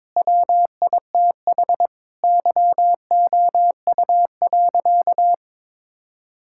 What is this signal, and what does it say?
Morse code, heard as a keyed tone, about 22 wpm, 700 Hz. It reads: WITH YOU.
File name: practice / 22wpm / Sentences with 4-Word Phrases - 22wpm